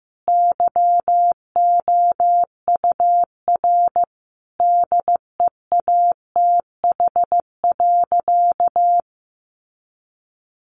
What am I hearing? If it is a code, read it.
Morse code, heard as a keyed tone, about 15 wpm, 700 Hz. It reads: YOUR DEATH.
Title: Sentences from Top 1500 Words 15wpm